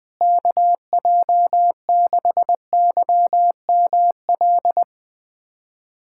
Morse code, heard as a keyed tone, about 21 wpm, 700 Hz. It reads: KJ6YML
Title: USA Callsign DE Callsign - 6 Characters - 20wpm